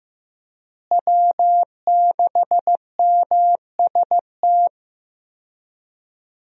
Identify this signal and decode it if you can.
Morse code, heard as a keyed tone, about 15 wpm, 700 Hz. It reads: W6MST
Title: USA Call Signs - 5 Characters - Repeated 3x - 15wpm